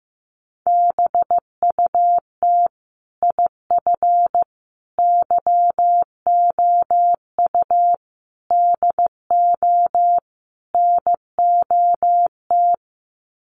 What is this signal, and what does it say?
Morse code, heard as a keyed tone, about 15 wpm, 700 Hz. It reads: BUT IF YOU DO NOT
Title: Sets of 5 Words Encore 15wpm